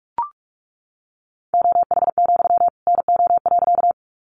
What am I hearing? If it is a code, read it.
Morse code, heard as a keyed tone, about 44 wpm, 700 Hz. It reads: OH, NO.
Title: Sentences from Top 500 Words 45wpm